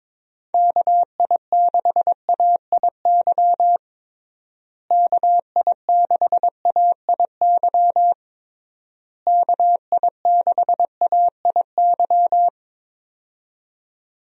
Morse code, heard as a keyed tone, about 22 wpm, 700 Hz. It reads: KI6AIY KI6AIY KI6AIY
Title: USA Call Signs - 6 Characters - Repeated 3x - 22wpm